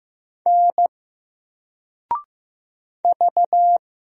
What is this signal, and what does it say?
Morse code, heard as a keyed tone, about 15 wpm, 700 Hz. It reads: N V
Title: Single Letter-Number Alt 15wpm